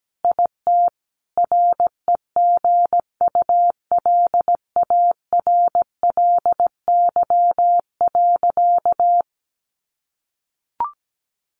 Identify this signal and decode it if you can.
Morse code, heard as a keyed tone, about 17 wpm, 700 Hz. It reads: IT REGULARLY.
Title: Sentences from Top 1400 Words 17wpm